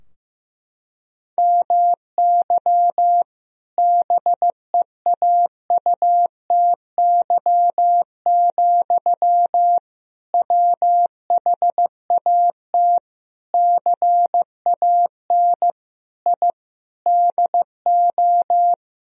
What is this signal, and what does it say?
Morse code, heard as a keyed tone, about 15 wpm, 700 Hz. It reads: MY BEAUTY, WHAT CAN I DO
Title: Sentences from Top 1000 Words - Review Part 2 - 15wpm